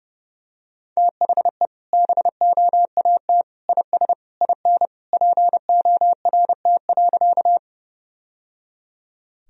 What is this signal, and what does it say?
Morse code, heard as a keyed tone, about 30 wpm, 700 Hz. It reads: THE BOAT IS IN PORT.
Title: Sentences from Top 1000 Words - Review Part 1 - 30wpm